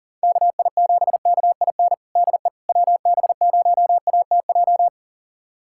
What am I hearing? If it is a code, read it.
Morse code, heard as a keyed tone, about 40 wpm, 700 Hz. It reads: KI7KIN DE WB0ATJ
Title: USA Callsign DE Callsign 40wpm